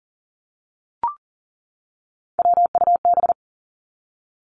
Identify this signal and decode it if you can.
Morse code, heard as a keyed tone, about 40 wpm, 700 Hz. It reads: WUB
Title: Three Challenging Letter-Number Combinations 40wpm